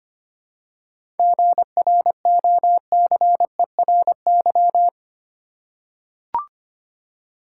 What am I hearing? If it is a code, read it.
Morse code, heard as a keyed tone, about 25 wpm, 700 Hz. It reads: GROCERY
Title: Seven Letter Words - 25wpm